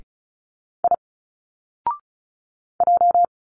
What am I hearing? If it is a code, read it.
Morse code, heard as a keyed tone, about 34 wpm, 700 Hz. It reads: I J
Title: Single Letters 35wpm